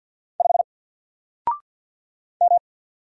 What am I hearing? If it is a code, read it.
Morse code, heard as a keyed tone, about 48 wpm, 700 Hz. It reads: 5 M
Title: Single Letter-Number Alt 50wpm